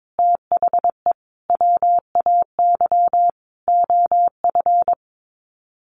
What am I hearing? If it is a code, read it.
Morse code, heard as a keyed tone, about 22 wpm, 700 Hz. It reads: THE WAY OF
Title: Sets of 5 Words Encore 22wpm